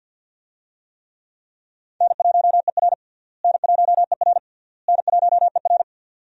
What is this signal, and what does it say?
Morse code, heard as a keyed tone, about 51 wpm, 700 Hz. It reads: N1ER N1ER N1ER
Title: USA Call Signs - 4 Characters - Repeated 3x - 50wpm